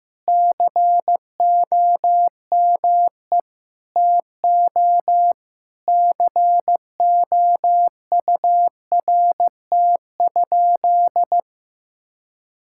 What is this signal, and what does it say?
Morse code, heard as a keyed tone, about 15 wpm, 700 Hz. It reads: COME TO COURT?
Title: Sentences from Top 1600 Words 15wpm